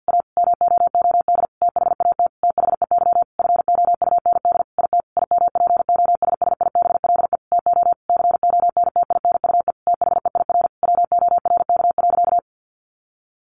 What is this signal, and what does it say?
Morse code, heard as a keyed tone, about 50 wpm, 700 Hz. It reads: A MOOD THAT THEY FOUND IT IMPOSSIBLE TO CONTINUE THEIR WORK.
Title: Sentences from Top 1100 Words 50wpm